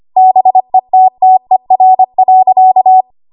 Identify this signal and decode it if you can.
Morse code, heard as a keyed tone, about 25 wpm, 750 Hz. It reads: BETTER.
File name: Morse / 25WPM